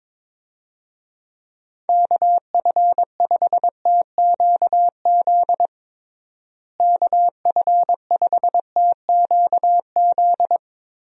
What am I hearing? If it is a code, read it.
Morse code, heard as a keyed tone, about 22 wpm, 700 Hz. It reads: KF5TQZ KF5TQZ
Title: USA Call Signs - 6 Characters - Repeated 2x - 22wpm